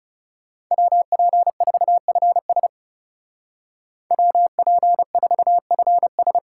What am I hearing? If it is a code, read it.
Morse code, heard as a keyed tone, about 39 wpm, 700 Hz. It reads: WP4FS WP4FS